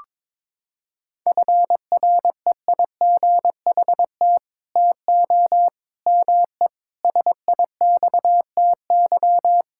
Morse code, heard as a keyed tone, about 22 wpm, 700 Hz. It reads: FREIGHT TO ME SIXTY
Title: Pangram Challenge v2 - 22wpm